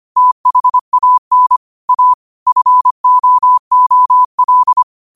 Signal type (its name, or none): Morse code